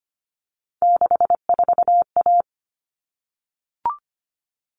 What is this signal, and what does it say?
Morse code, heard as a keyed tone, about 25 wpm, 700 Hz. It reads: 64A
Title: All Cut Numbers - 3-Digit 25wpm